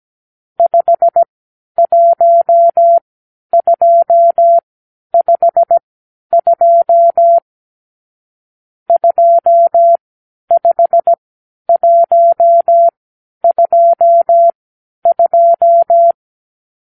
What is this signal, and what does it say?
Morse code, heard as a keyed tone, about 17 wpm, spaced as for 10 wpm, 680 Hz. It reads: 51252 25122
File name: arrl / Review (1-5)